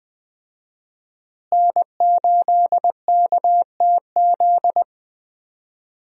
Morse code, heard as a keyed tone, about 20 wpm, 700 Hz. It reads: N8KTZ
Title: USA Call Signs - 5 Characters - Repeated 2x - 20wpm